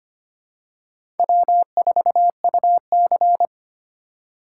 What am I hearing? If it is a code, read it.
Morse code, heard as a keyed tone, about 25 wpm, 700 Hz. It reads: W4UC
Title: USA Call Signs - 4 Characters - 3x Character Spacing - 25wpm